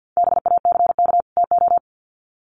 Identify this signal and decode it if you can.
Morse code, heard as a keyed tone, about 48 wpm, 700 Hz. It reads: BACK TO
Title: Sets of 2 Words 50wpm